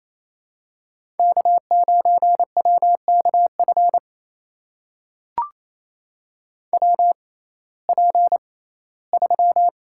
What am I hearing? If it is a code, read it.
Morse code, heard as a keyed tone, about 28 wpm, spaced as for 17 wpm, 700 Hz. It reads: K9WKF WP3
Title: USA Call Signs - 5 Characters - 6x Character Spacing - 28wpm